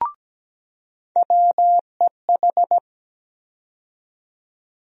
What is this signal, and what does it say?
Morse code, heard as a keyed tone, about 17 wpm, 700 Hz. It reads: WEH
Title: USA Call Sign Postfixes - 3 Letters 17wpm